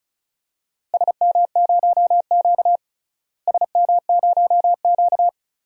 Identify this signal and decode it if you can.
Morse code, heard as a keyed tone, about 34 wpm, 700 Hz. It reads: SM0Q SM0Q
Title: DXCC-100 Call Signs - 4 Characters - Repeated 3x - 35wpm